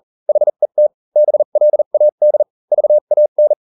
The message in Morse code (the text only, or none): HET BLAD VAN